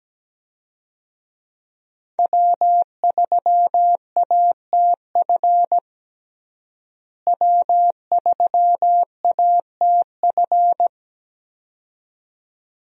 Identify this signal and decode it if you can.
Morse code, heard as a keyed tone, about 17 wpm, 700 Hz. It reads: W3ATF W3ATF
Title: USA Call Signs - 5 Characters - Repeated 2x - 17wpm